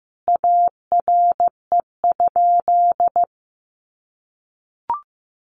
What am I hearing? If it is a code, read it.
Morse code, heard as a keyed tone, about 15 wpm, 700 Hz. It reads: ARE?
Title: A Sound Of Thunder 15wpm